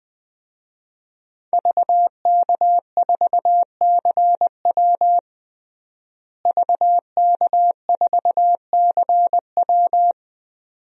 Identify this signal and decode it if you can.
Morse code, heard as a keyed tone, about 21 wpm, 700 Hz. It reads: VK4CW VK4CW
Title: DXCC-100 Call Signs - 5 Characters - Repeated 2x - 20wpm